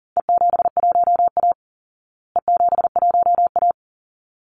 Morse code, heard as a keyed tone, about 37 wpm, 700 Hz. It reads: E71A E71A
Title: Call Signs Repeated 3x - 40wpm